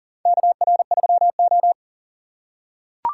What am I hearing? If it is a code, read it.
Morse code, heard as a keyed tone, about 37 wpm, 700 Hz. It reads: KR3O
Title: USA Call Signs - 4 Characters - 6x Character Spacing - 40wpm